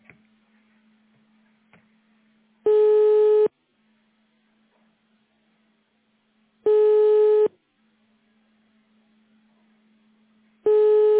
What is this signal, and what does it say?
Ringback tone (425 Hz, 1 s on, 3 s off).